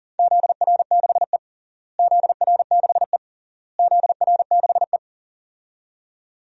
Morse code, heard as a keyed tone, about 40 wpm, 700 Hz. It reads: ZR6E ZR6E ZR6E
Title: DXCC-100 Call Signs - 4 Characters - Repeated 3x - 40wpm